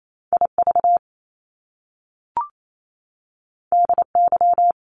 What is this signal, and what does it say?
Morse code, heard as a keyed tone, about 28 wpm, 700 Hz. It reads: IV DY
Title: Top Two Letter Combinations in Words 28wpm